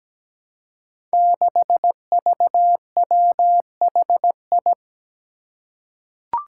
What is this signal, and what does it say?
Morse code, heard as a keyed tone, about 17 wpm, 700 Hz. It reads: 6VWHI
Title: Five Challenging Letter-Number Combinations 17wpm